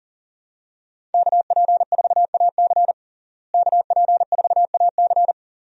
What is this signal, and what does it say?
Morse code, heard as a keyed tone, about 40 wpm, 700 Hz. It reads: KP4AC KP4AC
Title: USA Call Signs - 5 Characters - Repeated 3x - 40wpm